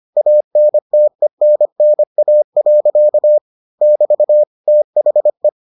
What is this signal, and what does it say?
Morse code, heard as a keyed tone, about 25 wpm, 600 Hz. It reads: ANTENNA. <BT> THE